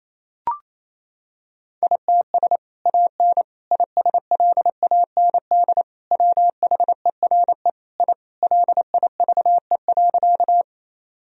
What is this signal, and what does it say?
Morse code, heard as a keyed tone, about 27 wpm, 700 Hz. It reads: ITS AN ISLAND WHERE I LIVE.
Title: Sentences from Top 1000 Words - Review Part 3 - 28wpm